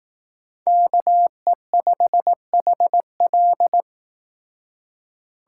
Morse code, heard as a keyed tone, about 18 wpm, 700 Hz. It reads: KE5HL
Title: Speed Racing - USA 2x2 Call Signs 15wpm